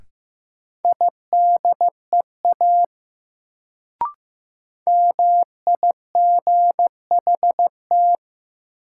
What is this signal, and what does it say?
Morse code, heard as a keyed tone, about 15 wpm, 700 Hz. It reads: IDEA MIGHT
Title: Sentences from Top 300 Words - New Word Drill - 15wpm